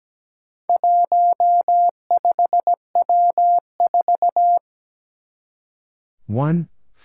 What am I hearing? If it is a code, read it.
Morse code, heard as a keyed tone, about 17 wpm, 700 Hz. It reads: 15W4
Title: Four Challenging Letter-Number Combinations 17wpm